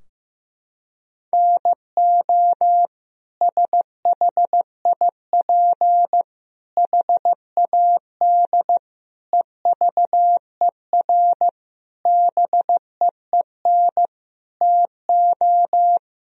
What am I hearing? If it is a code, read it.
Morse code, heard as a keyed tone, about 15 wpm, 700 Hz. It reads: NO SHIP HAD EVER BEEN TO
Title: Sentences from Top 1000 Words - Review Part 4 - 15wpm